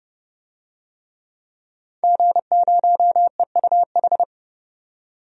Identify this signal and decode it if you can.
Morse code, heard as a keyed tone, about 31 wpm, 700 Hz. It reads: G0EUH
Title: DXCC-100 Call Signs - 5 Characters - 30wpm